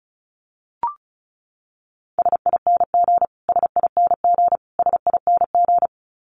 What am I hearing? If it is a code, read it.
Morse code, heard as a keyed tone, about 34 wpm, 700 Hz. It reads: SING SING SING
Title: Top 500 Words Repeated 35wpm